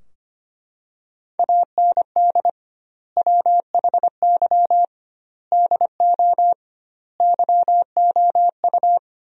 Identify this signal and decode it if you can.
Morse code, heard as a keyed tone, about 25 wpm, 700 Hz. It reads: AND WHY DO YOU
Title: 2x Word-Spacing - Sentences from Top 400 Words - 25wpm